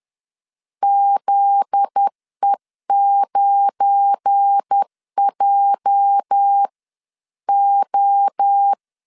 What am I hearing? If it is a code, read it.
Morse code, heard as a keyed tone, about 10 wpm, spaced as for 9 wpm, 800 Hz. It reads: ZE9JO
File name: LEZIONI / 8 YJ45 / 8 YJ 10wpm 2651